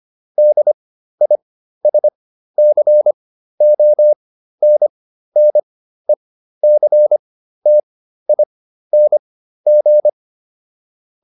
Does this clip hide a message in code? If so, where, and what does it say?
Yes: Morse code from 0.4 to 10.1 s, heard as a keyed tone, about 25 wpm, spaced as for 13 wpm, 600 Hz. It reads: DISCONNECTING